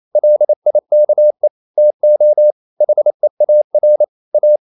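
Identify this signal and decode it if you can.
Morse code, heard as a keyed tone, about 28 wpm, 600 Hz. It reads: LIKE TO HEAR A
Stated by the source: Morse code practice at 28 wpm, text of Ken Follett's novel Never in English, from CW med Gnister.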